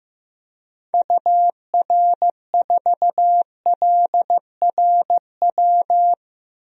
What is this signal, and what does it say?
Morse code, heard as a keyed tone, about 15 wpm, 700 Hz. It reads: UR4LRW
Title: DXCC-100 Call Signs - 6 Characters - Repeated 3x - 15wpm